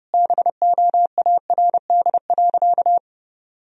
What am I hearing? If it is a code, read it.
Morse code, heard as a keyed tone, about 32 wpm, 700 Hz. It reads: BOARD.
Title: Sentences from Top 1000 Words - Review Part 4 - 30wpm